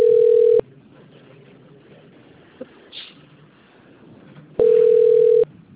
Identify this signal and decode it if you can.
Ringback tone (North America), 440 + 480 Hz.
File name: out- - - - - .13110